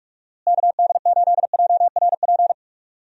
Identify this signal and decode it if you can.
Morse code, heard as a keyed tone, about 46 wpm, 700 Hz. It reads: KD8JRP
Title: USA Call Signs - 6 Characters - Repeated 3x - 45wpm